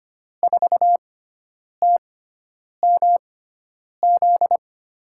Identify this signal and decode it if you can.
Morse code, heard as a keyed tone, about 25 wpm, spaced as for 9 wpm, 700 Hz. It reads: 4TMZ
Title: USA Call Signs - 6 Characters - 6x Character Spacing - 25wpm